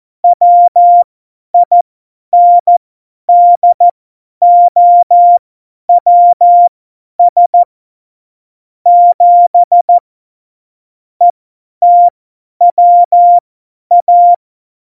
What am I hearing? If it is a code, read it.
Morse code, heard as a keyed tone, about 14 wpm, spaced as for 10 wpm, 700 Hz. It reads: WINDOWS 7 ETWA
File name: Morsekurs / Witze / Witz 15. Januar 2019, 50 F70